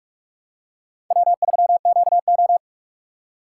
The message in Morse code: W3QO

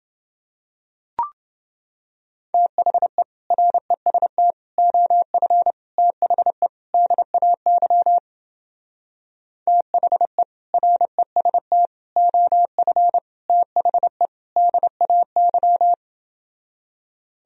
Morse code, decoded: THE REST OF THE DAY THE REST OF THE DAY

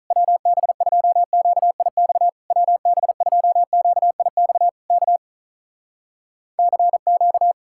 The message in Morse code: WB2QIX WB2QIX K CQ

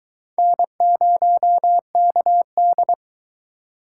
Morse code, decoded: N0KD

